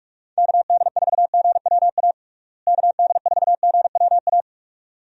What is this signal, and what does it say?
Morse code, heard as a keyed tone, about 45 wpm, 700 Hz. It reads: KD4GWA KD4GWA
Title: USA Call Signs - 6 Characters - Repeated 2x - 45wpm